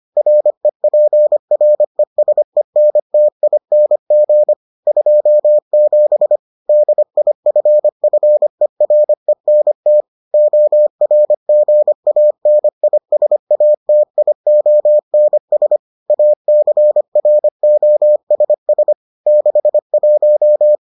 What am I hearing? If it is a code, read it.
Morse code, heard as a keyed tone, about 25 wpm, 600 Hz. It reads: REPRESENTING 27 DIFFERENT ORGANISATIONS ACROSS 61